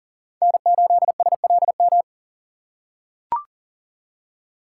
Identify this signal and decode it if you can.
Morse code, heard as a keyed tone, about 40 wpm, 700 Hz. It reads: N8SLM
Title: USA Call Signs - 5 Characters - 3x Character Spacing - 40wpm